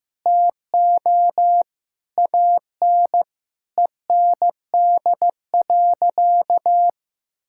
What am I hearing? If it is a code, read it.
Morse code, heard as a keyed tone, about 15 wpm, 700 Hz. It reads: TO AN END.
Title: Sentences from Top 1000 Words - Review Part 6 - 15wpm